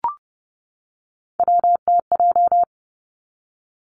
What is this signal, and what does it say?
Morse code, heard as a keyed tone, about 30 wpm, 700 Hz. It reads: WTJ
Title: Three Challenging Letter-Number Combinations 30wpm